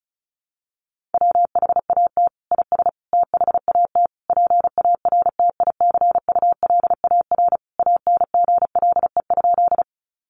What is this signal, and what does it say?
Morse code, heard as a keyed tone, about 34 wpm, 700 Hz. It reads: WHAT IS THAT PARTICULAR ANGLE?
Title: Sentences from Top 1500 Words 35wpm